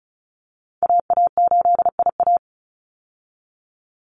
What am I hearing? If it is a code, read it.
Morse code, heard as a keyed tone, about 34 wpm, 700 Hz. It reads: AA8IA
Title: Call Signs Repeated 2x - 35wpm